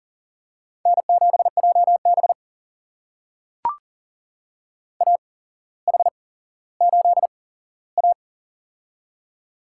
Morse code, heard as a keyed tone, about 40 wpm, spaced as for 17 wpm, 700 Hz. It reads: N7JB AH8A
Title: USA Call Signs - 4 Characters - 8x Character Spacing - 40wpm